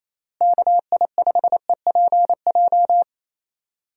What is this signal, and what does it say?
Morse code, heard as a keyed tone, about 28 wpm, 700 Hz. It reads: KI5EPJ
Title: USA Call Signs - 6 Characters - Repeated 2x - 28wpm